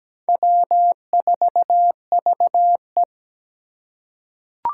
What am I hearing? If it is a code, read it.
Morse code, heard as a keyed tone, about 17 wpm, 700 Hz. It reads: W4VE
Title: Four Challenging Letter-Number Combinations 17wpm